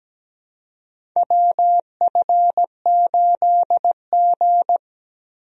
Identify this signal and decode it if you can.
Morse code, heard as a keyed tone, about 17 wpm, 700 Hz. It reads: WF8G
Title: USA Call Signs - 4 Characters - Repeated 3x - 17wpm